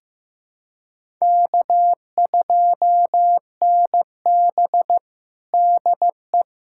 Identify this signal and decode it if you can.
Morse code, heard as a keyed tone, about 15 wpm, 700 Hz. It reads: K2NB DE
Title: USA Callsign DE Callsign - 4 Characters - 15wpm